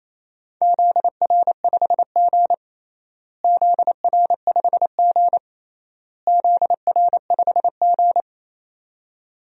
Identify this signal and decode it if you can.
Morse code, heard as a keyed tone, about 28 wpm, 700 Hz. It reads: ZR5G ZR5G ZR5G
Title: DXCC-100 Call Signs - 4 Characters - Repeated 3x - 28wpm